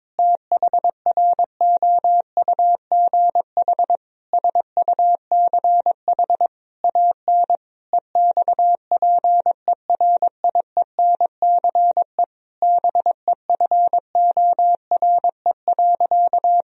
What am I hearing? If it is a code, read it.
Morse code, heard as a keyed tone, about 22 wpm, 700 Hz. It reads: THROUGH SUCH AN EXPERIENCE BEFORE.
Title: Sentences from Top 1000 Words - Review Part 3 - 22wpm